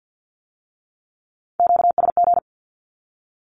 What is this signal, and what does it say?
Morse code, heard as a keyed tone, about 50 wpm, 700 Hz. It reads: QSG
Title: Q-Codes with Spoken Meaning - 50wpm